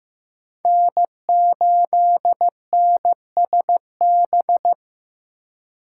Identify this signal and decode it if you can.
Morse code, heard as a keyed tone, about 15 wpm, 700 Hz. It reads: N8NSB
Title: USA Call Signs - 5 Characters - 3x Character Spacing - 15wpm